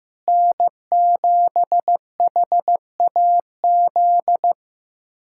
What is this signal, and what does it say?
Morse code, heard as a keyed tone, about 15 wpm, 700 Hz. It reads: N7HAZ
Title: USA Callsign DE Callsign - 5 Characters - 15wpm